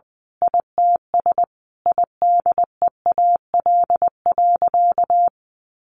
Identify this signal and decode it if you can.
Morse code, heard as a keyed tone, about 20 wpm, 700 Hz. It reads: ITS IDEAL.
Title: Sentences from Top 1400 Words 20wpm